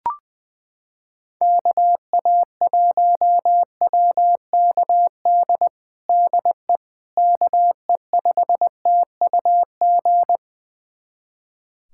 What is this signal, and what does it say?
Morse code, heard as a keyed tone, about 20 wpm, 700 Hz. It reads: KA1WKD DE KE5TUG
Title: USA Callsign DE Callsign - 6 Characters - 20wpm